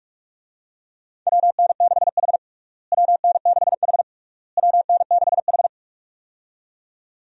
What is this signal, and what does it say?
Morse code, heard as a keyed tone, about 46 wpm, 700 Hz. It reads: WN6H WN6H WN6H
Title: USA Call Signs - 4 Characters - Repeated 3x - 45wpm